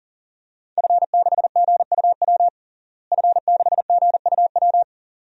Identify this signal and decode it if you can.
Morse code, heard as a keyed tone, about 40 wpm, 700 Hz. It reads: F6GUW F6GUW
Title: DXCC-100 Call Signs - 5 Characters - Repeated 2x - 40wpm